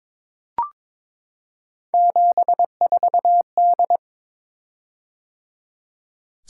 Morse code, heard as a keyed tone, about 22 wpm, 700 Hz. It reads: 74D
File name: practice / 22wpm / Three Letter-Number Combinations - 22wpm